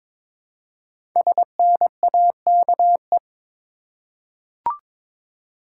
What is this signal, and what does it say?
Morse code, heard as a keyed tone, about 22 wpm, 700 Hz. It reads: SNAKE
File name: 5 Letter Words - First 3 Letters Given - 22wpm